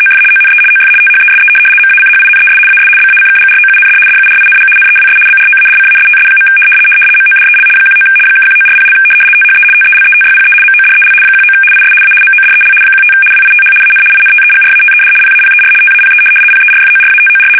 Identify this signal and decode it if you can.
RTTY, about 75 baud. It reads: OF ONE OF THESE MACHINES AS BACKGROUND DURING NEWS BROADCASTS. MODEL 19 (1940) THE TELETYPE MODEL 19 IS A MODEL 15 WITH AN INTEGRATED PAPER TAPE PERFORATOR
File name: fsk850-75B / teletype-corp